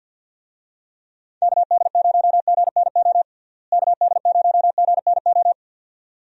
Morse code, heard as a keyed tone, about 51 wpm, 700 Hz. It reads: KD0GNO KD0GNO
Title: USA Call Signs - 6 Characters - Repeated 2x - 50wpm